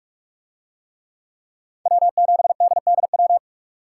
Morse code, heard as a keyed tone, about 43 wpm, 700 Hz. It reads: W7DDW